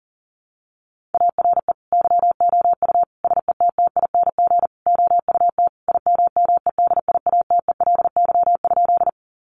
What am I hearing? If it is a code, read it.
Morse code, heard as a keyed tone, about 37 wpm, 700 Hz. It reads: ARE YOU SETTING OUT IMMEDIATELY?
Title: Sentences from Top 1300 Words 40wpm